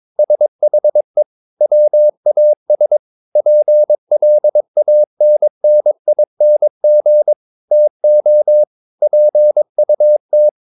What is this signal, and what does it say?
Morse code, heard as a keyed tone, about 22 wpm, 600 Hz. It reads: SHE WAS PLANNING TO PUT
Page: CW med Gnister